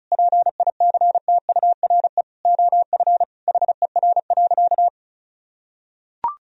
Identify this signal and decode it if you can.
Morse code, heard as a keyed tone, about 36 wpm, 700 Hz. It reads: PICTURE OF HER.
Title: Sentences from Top 1000 Words - Review Part 1 - 35wpm